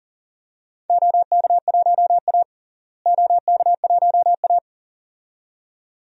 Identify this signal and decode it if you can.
Morse code, heard as a keyed tone, about 40 wpm, 700 Hz. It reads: OK1A OK1A
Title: DXCC-100 Call Signs - 4 Characters - Repeated 2x - 40wpm